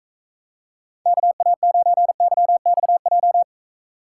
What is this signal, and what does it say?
Morse code, heard as a keyed tone, about 45 wpm, 700 Hz. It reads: KA9YXJ